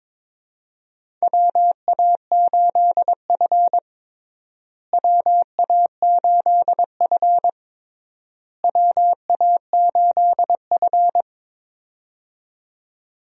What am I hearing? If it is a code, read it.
Morse code, heard as a keyed tone, about 22 wpm, 700 Hz. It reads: WA8F WA8F WA8F
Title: USA Call Signs - 4 Characters - Repeated 3x - 22wpm